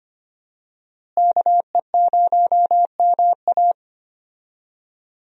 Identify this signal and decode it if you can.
Morse code, heard as a keyed tone, about 25 wpm, 700 Hz. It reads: KE0MA
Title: USA Call Signs - 5 Characters - Repeated 3x - 25wpm